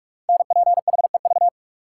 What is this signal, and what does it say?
Morse code, heard as a keyed tone, about 44 wpm, 700 Hz. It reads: NPHEV
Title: Code Groups - 5 Letters - 45wpm